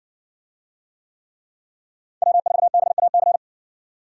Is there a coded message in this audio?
Yes: Morse code from 2.2 to 3.4 s, heard as a keyed tone, about 58 wpm, 700 Hz. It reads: W4BAC